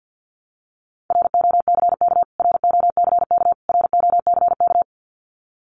Morse code, heard as a keyed tone, about 50 wpm, 700 Hz. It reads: ROCK ROCK ROCK
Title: Top 500 Words Repeated 50wpm